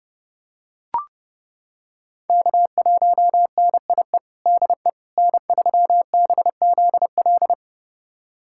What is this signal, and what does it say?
Morse code, heard as a keyed tone, about 30 wpm, 700 Hz. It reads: K1NIE DE N3BZL
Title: USA Callsign DE Callsign - 5 Characters - 30wpm